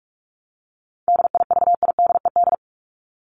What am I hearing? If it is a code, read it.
Morse code, heard as a keyed tone, about 44 wpm, 700 Hz. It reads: DIVIDED
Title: ED Suffix - Top 100 Words with Base Words 45wpm